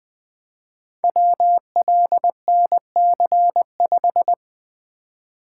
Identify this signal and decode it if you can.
Morse code, heard as a keyed tone, about 20 wpm, 700 Hz. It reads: WLNC5
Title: Code Groups - 5 Letter:Numbers - 20wpm